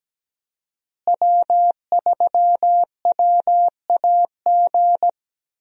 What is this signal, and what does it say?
Morse code, heard as a keyed tone, about 17 wpm, 700 Hz. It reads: W3WAG
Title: USA Call Signs - 5 Characters - Repeated 2x - 17wpm